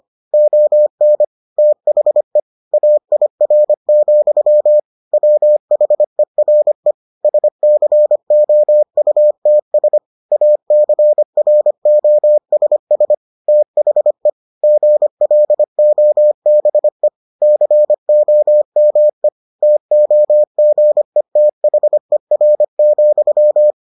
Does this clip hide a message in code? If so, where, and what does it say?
Yes: Morse code from 0.3 to 23.7 s, heard as a keyed tone, about 25 wpm, 600 Hz. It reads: ON THE AIR, WHERE SCOUTS ACROSS THE GLOBE COME TOGETHER,